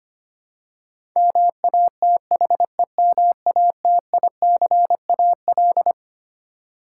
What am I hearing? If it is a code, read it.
Morse code, heard as a keyed tone, about 25 wpm, 700 Hz. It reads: MATHEMATICAL
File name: Twelve Letter Words - 25wpm